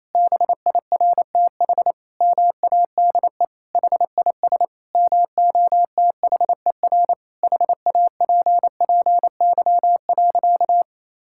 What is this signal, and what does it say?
Morse code, heard as a keyed tone, about 28 wpm, 700 Hz. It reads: BIRTH MADE HIS MOTHER HAPPY.
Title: Sentences from Top 1000 Words - Review Part 5 - 28wpm